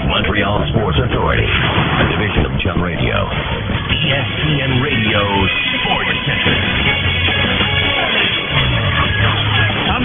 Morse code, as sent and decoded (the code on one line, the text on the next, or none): none